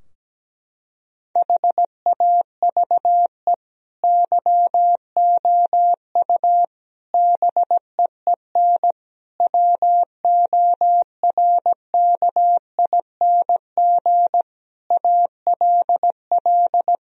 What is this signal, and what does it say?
Morse code, heard as a keyed tone, about 17 wpm, 700 Hz. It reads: HAVE YOU BEEN WORKING ALL
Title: Sentences from Top 1000 Words - Review Part 4 - 17wpm